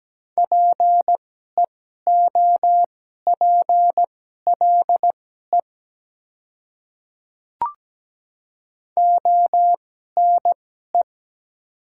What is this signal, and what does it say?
Morse code, heard as a keyed tone, about 17 wpm, spaced as for 12 wpm, 700 Hz. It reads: PEOPLE ONE
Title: 2x Character Spacing - Top 100 Words - 17wpm